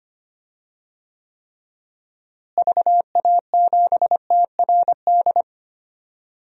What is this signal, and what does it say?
Morse code, heard as a keyed tone, about 25 wpm, 700 Hz. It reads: VA7TRD
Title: DXCC-100 Call Signs - 6 Characters - Repeated 2x - 25wpm